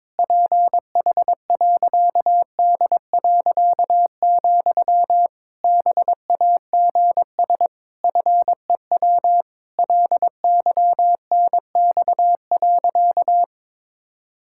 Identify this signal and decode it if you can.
Morse code, heard as a keyed tone, about 22 wpm, 700 Hz. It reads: PH.D., BAGS FEW LYNX.
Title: Pangram Challenge v2 - 22wpm